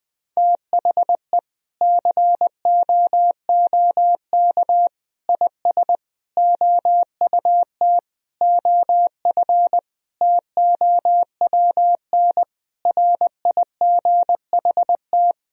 Morse code, heard as a keyed tone, about 20 wpm, 700 Hz. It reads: THE COOK IS OUT OF TOWN RIGHT
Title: Sentences from Top 1000 Words - Review Part 4 - 20wpm